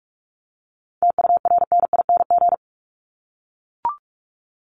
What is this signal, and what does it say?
Morse code, heard as a keyed tone, about 44 wpm, 700 Hz. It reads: TURNING
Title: ING Suffix - Top 100 Words with Base Words 45wpm